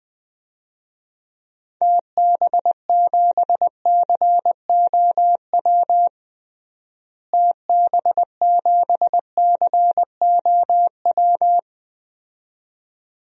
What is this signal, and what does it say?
Morse code, heard as a keyed tone, about 20 wpm, 700 Hz. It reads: TB7COW TB7COW
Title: DXCC-100 Call Signs - 6 Characters - Repeated 2x - 20wpm